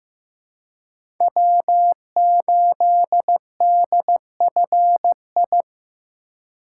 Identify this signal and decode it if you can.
Morse code, heard as a keyed tone, about 15 wpm, 700 Hz. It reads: W8DFI
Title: USA Call Signs - 5 Characters - 3x Character Spacing - 15wpm